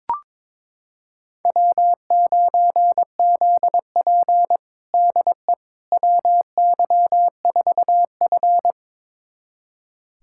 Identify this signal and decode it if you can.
Morse code, heard as a keyed tone, about 22 wpm, 700 Hz. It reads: W9ZP DE WY4F
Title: USA Callsign DE Callsign - 4 Characters - 22wpm